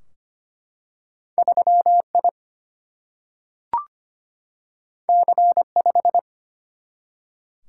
Morse code, heard as a keyed tone, about 25 wpm, 700 Hz. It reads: 3I C5
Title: Single-Letter Single-Number 25wpm